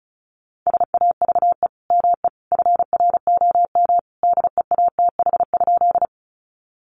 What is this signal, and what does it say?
Morse code, heard as a keyed tone, about 34 wpm, 700 Hz. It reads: SAVE ME FROM DEATH?
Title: Sentences from Top 1100 Words 35wpm